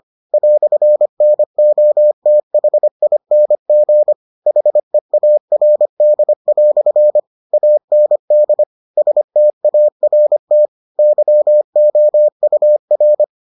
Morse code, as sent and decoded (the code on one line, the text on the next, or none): .-..-. -. --- - .... .. -. --. / .... .- .-. -.. .-..-. / .- -. -.. / ... - .- .-. - / -.-- --- ..- .-.
"NOTHING HEARD" AND START YOUR